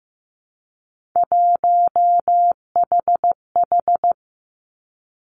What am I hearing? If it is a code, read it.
Morse code, heard as a keyed tone, about 15 wpm, 700 Hz. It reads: 1HH
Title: Three Challenging Letter-Number Combinations 15wpm